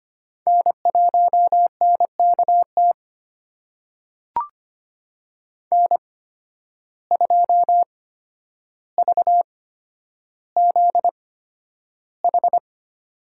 Morse code, heard as a keyed tone, about 25 wpm, spaced as for 11 wpm, 700 Hz. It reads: N1NKT N2VZH
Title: USA Call Signs - 5 Characters - 8x Character Spacing - 25wpm